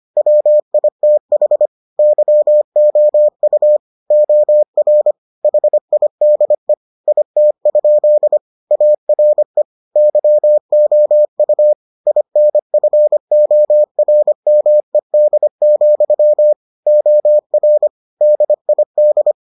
Morse code, heard as a keyed tone, about 25 wpm, 600 Hz. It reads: WITH YOU OR HIDE IT? ARE YOU INFORMED, OR DID